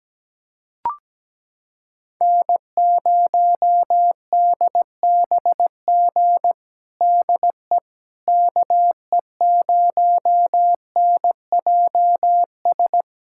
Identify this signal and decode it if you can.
Morse code, heard as a keyed tone, about 17 wpm, 700 Hz. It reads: N0DBG DE KE0NJS
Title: USA Callsign DE Callsign 17wpm